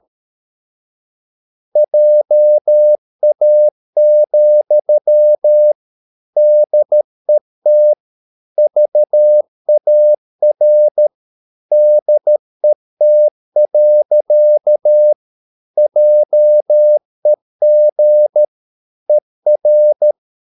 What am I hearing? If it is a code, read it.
Morse code, heard as a keyed tone, about 13 wpm, 600 Hz. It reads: JA, DET VAR DET. JEG ER
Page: Ferie 13 wpm